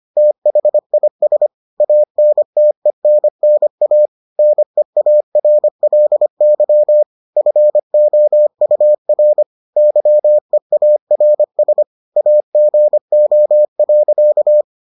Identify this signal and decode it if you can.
Morse code, heard as a keyed tone, about 25 wpm, 600 Hz. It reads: THIS ANTENNA NEARLY FOUR YEARS AGO.